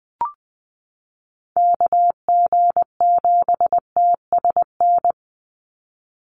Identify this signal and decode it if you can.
Morse code, heard as a keyed tone, about 20 wpm, 700 Hz. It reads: KG7TSN
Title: USA Call Signs 20wpm